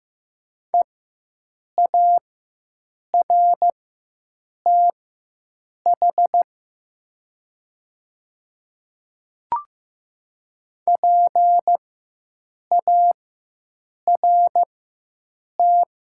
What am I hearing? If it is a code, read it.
Morse code, heard as a keyed tone, about 15 wpm, spaced as for 7 wpm, 700 Hz. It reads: EARTH PART
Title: 4x Character Spacing - Top 200 Words - 15wpm